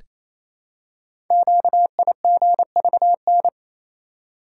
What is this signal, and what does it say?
Morse code, heard as a keyed tone, about 27 wpm, 700 Hz. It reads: QIGVN
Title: Code Groups - 5 Letters - 28wpm